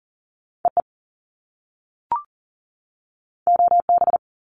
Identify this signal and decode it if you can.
Morse code, heard as a keyed tone, about 40 wpm, 700 Hz. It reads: EE OB
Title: Top Two Letter Combinations in Words 40wpm